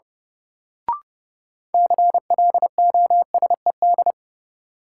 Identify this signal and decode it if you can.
Morse code, heard as a keyed tone, about 32 wpm, 700 Hz. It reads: CLOSED
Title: Sentences from Top 1400 Words - New Word Drill - 30wpm